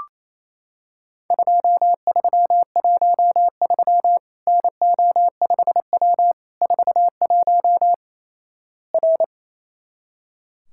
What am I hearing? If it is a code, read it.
Morse code, heard as a keyed tone, about 28 wpm, 700 Hz. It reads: 2313 NO5W 41 R
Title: Worked All Europe DX Contest - QTC Exchange - 28wpm